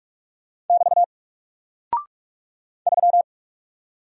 Morse code, heard as a keyed tone, about 44 wpm, 700 Hz. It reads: <BT> 3
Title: Single Character (Letter:Number:Punctuation:Prosign) Alt 45wpm